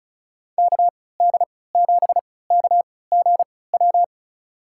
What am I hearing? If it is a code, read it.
Morse code, heard as a keyed tone, about 36 wpm, spaced as for 20 wpm, 700 Hz. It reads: KD7KGW
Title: USA Call Signs - 6 Characters - 3x Character Spacing - 35wpm